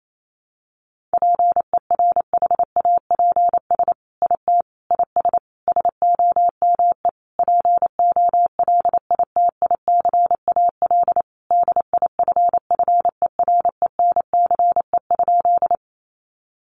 Morse code, heard as a keyed tone, about 28 wpm, 700 Hz. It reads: PERHAPS IT IS SOME POLITICAL DIFFERENCE?
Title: Sentences from Top 800 Words 28wpm